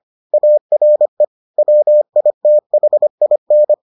Morse code, heard as a keyed tone, about 25 wpm, 600 Hz. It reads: ARE WITHIN